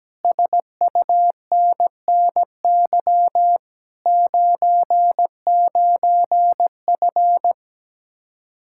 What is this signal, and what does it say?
Morse code, heard as a keyed tone, about 17 wpm, 700 Hz. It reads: SUNNY 99F
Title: Speed-Racing Common QSO Phrases Encore - 17wpm